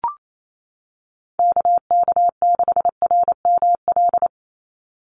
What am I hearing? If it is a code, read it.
Morse code, heard as a keyed tone, about 28 wpm, 700 Hz. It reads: KK6RML
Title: USA Call Signs 28wpm